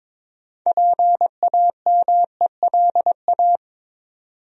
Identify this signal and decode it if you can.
Morse code, heard as a keyed tone, about 21 wpm, 700 Hz. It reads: PAMELA
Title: USA Common Names - 22wpm